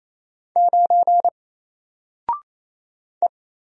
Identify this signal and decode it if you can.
Morse code, heard as a keyed tone, about 28 wpm, 700 Hz. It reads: 9 E
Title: Single Character (Letter:Number:Punctuation:Prosign) Alt 28wpm